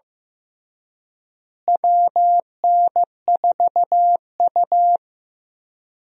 Morse code, heard as a keyed tone, about 15 wpm, 700 Hz. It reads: WN4U
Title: USA Call Signs - 4 Characters - Repeated 2x - 15wpm